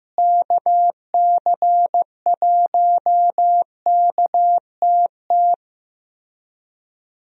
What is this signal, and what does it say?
Morse code, heard as a keyed tone, about 14 wpm, 700 Hz. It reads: KC1KTT
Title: USA Call Signs - 6 Characters - Repeated 3x - 15wpm